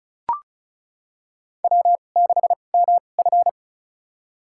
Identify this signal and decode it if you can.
Morse code, heard as a keyed tone, about 34 wpm, spaced as for 25 wpm, 700 Hz. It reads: W6MF
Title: USA Call Signs - 4 Characters - 2x Character Spacing - 35wpm